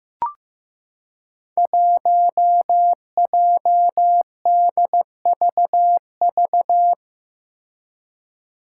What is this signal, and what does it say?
Morse code, heard as a keyed tone, about 15 wpm, 700 Hz. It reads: 1JDVV
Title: Five Challenging Letter-Number Combinations 15wpm